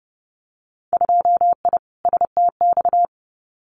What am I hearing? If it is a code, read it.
Morse code, heard as a keyed tone, about 30 wpm, 700 Hz. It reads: 2I STX
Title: Winter Field Day Exchanges 30wpm